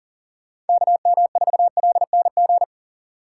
Morse code, heard as a keyed tone, about 40 wpm, 700 Hz. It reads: KM4LNG
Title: USA Call Signs - 6 Characters - 2x Character Spacing - 40wpm